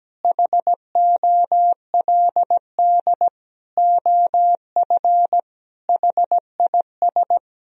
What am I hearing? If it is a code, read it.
Morse code, heard as a keyed tone, about 17 wpm, 700 Hz. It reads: HOLD OF HIS
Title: Sentences from Top 1000 Words - Review Part 2 - 17wpm